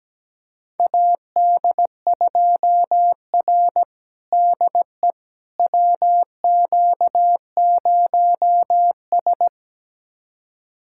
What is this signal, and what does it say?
Morse code, heard as a keyed tone, about 17 wpm, 700 Hz. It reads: AD2R DE WQ0S
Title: USA Callsign DE Callsign - 4 Characters - 17wpm